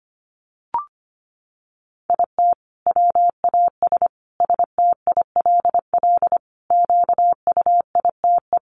Morse code, heard as a keyed tone, about 25 wpm, 700 Hz. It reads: IT WAS STILL QUITE
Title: Sentences from Top 1500 Words 25wpm